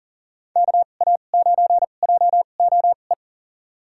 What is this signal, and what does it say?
Morse code, heard as a keyed tone, about 40 wpm, spaced as for 29 wpm, 700 Hz. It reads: KA9JOE
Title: USA Call Signs - 2x Character Spacing - 40wpm